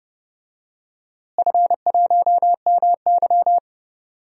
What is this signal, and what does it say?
Morse code, heard as a keyed tone, about 30 wpm, 700 Hz. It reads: F1MY
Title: DXCC-100 Call Signs - 4 Characters - Repeated 2x - 30wpm